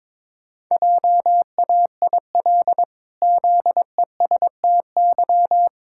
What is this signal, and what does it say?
Morse code, heard as a keyed tone, about 22 wpm, 700 Hz. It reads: JAIL ZESTY
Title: Pangram Challenge v2 - 22wpm